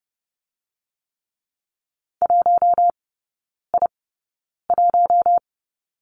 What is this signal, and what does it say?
Morse code, heard as a keyed tone, about 30 wpm, 700 Hz. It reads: 1 I 1
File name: 081 Lesson 14e - Number 2 - Three Characters 30wpm